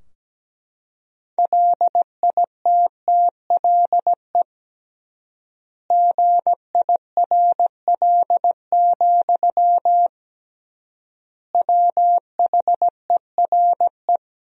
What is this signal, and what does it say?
Morse code, heard as a keyed tone, about 17 wpm, 700 Hz. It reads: LITTLE GIRL, WHERE
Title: 3x Word-Spacing - Sentences from Top 400 Words - 17wpm